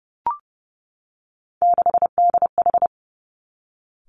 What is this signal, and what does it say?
Morse code, heard as a keyed tone, about 30 wpm, 700 Hz. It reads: 6DH
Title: Three Challenging Letter-Number Combinations 30wpm